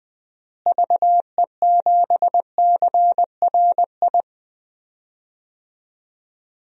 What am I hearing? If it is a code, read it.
Morse code, heard as a keyed tone, about 20 wpm, 700 Hz. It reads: VE7CRI
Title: DXCC-100 Call Signs - 6 Characters - Repeated 3x - 20wpm